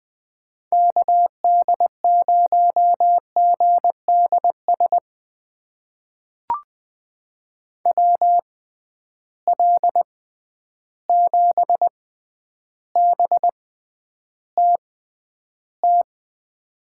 Morse code, heard as a keyed tone, about 20 wpm, spaced as for 10 wpm, 700 Hz. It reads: KD0GDS WL7BTT
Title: USA Call Signs - 6 Characters - 6x Character Spacing - 20wpm